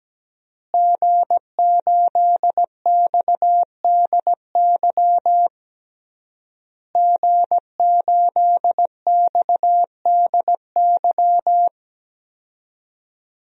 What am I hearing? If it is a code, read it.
Morse code, heard as a keyed tone, about 17 wpm, 700 Hz. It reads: G8XDY G8XDY
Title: DXCC-100 Call Signs - 5 Characters - Repeated 3x - 17wpm